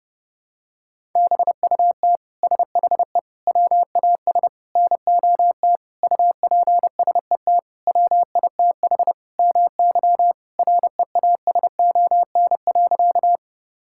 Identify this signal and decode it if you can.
Morse code, heard as a keyed tone, about 32 wpm, 700 Hz. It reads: BUT SHE WAS NOT UPSET WITH MY REASON.
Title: Sentences from Top 1000 Words - Review Part 5 - 30wpm